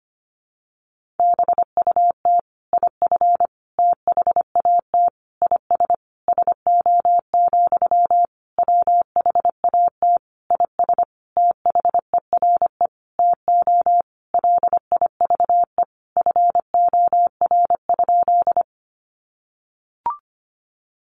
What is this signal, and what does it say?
Morse code, heard as a keyed tone, about 25 wpm, 700 Hz. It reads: BUT IF THAT IS SO, WHAT IS THERE TO LIVE FOR?
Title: Sentences from Top 200 Words 25wpm